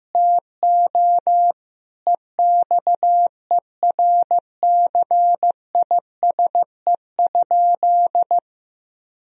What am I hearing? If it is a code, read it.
Morse code, heard as a keyed tone, about 16 wpm, 700 Hz. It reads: TO EXERCISE?
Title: Sentences from Top 1000 Words - Review Part 4 - 15wpm